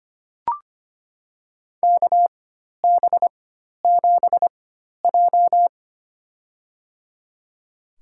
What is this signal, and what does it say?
Morse code, heard as a keyed tone, about 25 wpm, spaced as for 12 wpm, 700 Hz. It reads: KB7J
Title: USA Call Signs - 4 Characters - 4x Character Spacing - 25wpm